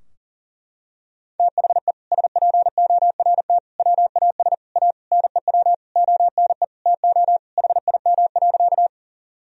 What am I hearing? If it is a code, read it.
Morse code, heard as a keyed tone, about 37 wpm, 700 Hz. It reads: THE SPORT WAS A NEW ONE TO HIM.
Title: Sentences from Top 1000 Words - Review Part 4 - 40wpm